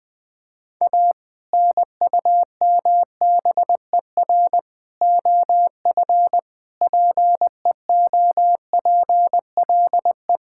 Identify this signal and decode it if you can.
Morse code, heard as a keyed tone, about 20 wpm, 700 Hz. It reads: A NUMBER OF PEOPLE
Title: Sentences with 3-Word Phrases - 20wpm